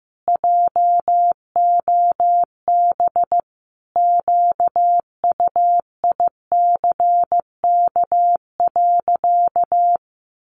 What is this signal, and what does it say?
Morse code, heard as a keyed tone, about 15 wpm, 700 Hz. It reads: JOB QUICK.
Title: Sentences from Top 600 Words 15wpm